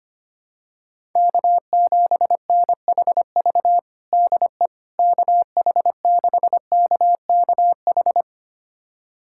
Morse code, heard as a keyed tone, about 25 wpm, 700 Hz. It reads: K7NHV DE KH6KKH
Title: USA Callsign DE Callsign 25wpm